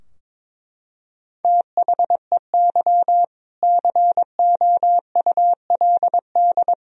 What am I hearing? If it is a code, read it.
Morse code, heard as a keyed tone, about 22 wpm, 700 Hz. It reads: THEY COULD
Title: Sentences from Top 1000 Words - Review Part 3 - 22wpm